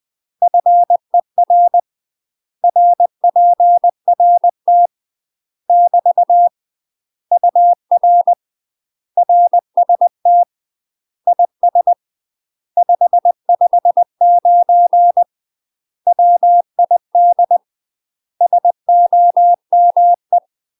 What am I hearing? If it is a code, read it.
Morse code, heard as a keyed tone, about 20 wpm, 700 Hz. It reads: FER RPRT <BT> UR RST IS 559 WID SOME
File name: UK MASTER QSOs TRAINING / 20WPM / UK MASTER QSOs 0023